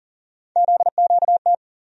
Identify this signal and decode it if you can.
Morse code, heard as a keyed tone, about 40 wpm, 700 Hz. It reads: ZQT
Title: USA Call Sign Postfixes - 3 Letters 40wpm